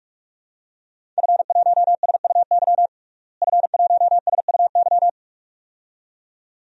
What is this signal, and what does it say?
Morse code, heard as a keyed tone, about 46 wpm, 700 Hz. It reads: F1SUY F1SUY
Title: DXCC-100 Call Signs - 5 Characters - Repeated 2x - 45wpm